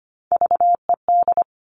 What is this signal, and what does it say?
Morse code, heard as a keyed tone, about 25 wpm, 700 Hz. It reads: VED
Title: Top Three Letter Combinations in Words 25wpm